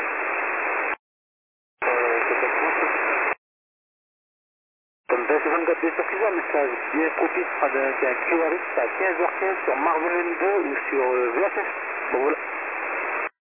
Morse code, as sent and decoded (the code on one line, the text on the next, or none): none